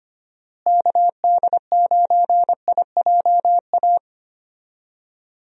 Morse code, heard as a keyed tone, about 25 wpm, 700 Hz. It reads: KD9IJA
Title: USA Call Signs - 6 Characters - Repeated 3x - 25wpm